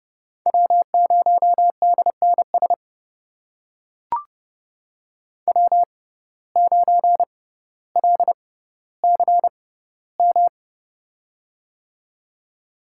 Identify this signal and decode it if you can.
Morse code, heard as a keyed tone, about 30 wpm, spaced as for 15 wpm, 700 Hz. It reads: W0DNS W9LCM
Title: USA Call Signs - 5 Characters - 6x Character Spacing - 30wpm